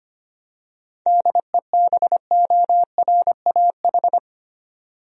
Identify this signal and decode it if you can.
Morse code, heard as a keyed tone, about 25 wpm, 700 Hz. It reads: DEBORAH
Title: USA Common Names - 25wpm